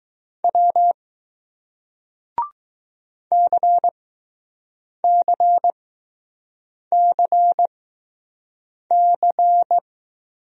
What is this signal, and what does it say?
Morse code, heard as a keyed tone, about 20 wpm, spaced as for 6 wpm, 700 Hz. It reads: W CCCC